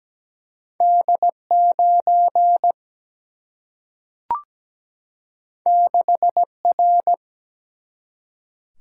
Morse code, heard as a keyed tone, about 17 wpm, 700 Hz. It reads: D 9 6 R
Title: Single-Letter Single-Number 17wpm